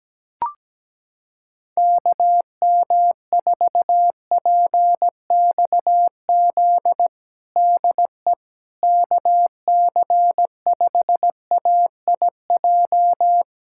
Morse code, heard as a keyed tone, about 17 wpm, 700 Hz. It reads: KM4PXZ DE KC5AIJ
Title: USA Callsign DE Callsign - 6 Characters - 17wpm